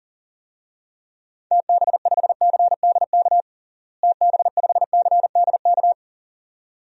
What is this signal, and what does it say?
Morse code, heard as a keyed tone, about 40 wpm, 700 Hz. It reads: TB5CDK TB5CDK
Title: DXCC-100 Call Signs - 6 Characters - Repeated 2x - 40wpm